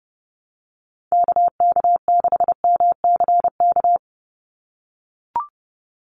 Morse code, heard as a keyed tone, about 30 wpm, 700 Hz. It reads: KK6MCK
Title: USA Call Signs 30wpm